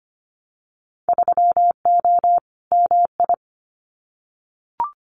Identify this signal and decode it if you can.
Morse code, heard as a keyed tone, about 25 wpm, 700 Hz. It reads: 3O MI
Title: Winter Field Day Exchanges 25wpm